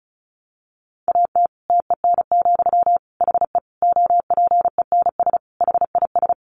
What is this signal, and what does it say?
Morse code, heard as a keyed tone, about 34 wpm, 700 Hz. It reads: AT TEN, HE OPENS HIS
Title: Easy Phrases 35wpm